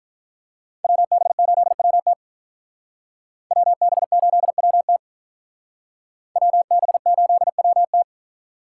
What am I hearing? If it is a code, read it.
Morse code, heard as a keyed tone, about 45 wpm, 700 Hz. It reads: WB8WT WB8WT WB8WT